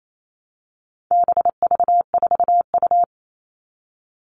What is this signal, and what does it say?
Morse code, heard as a keyed tone, about 28 wpm, 700 Hz. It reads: BV4U
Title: All Cut Numbers - 4-Digit 28wpm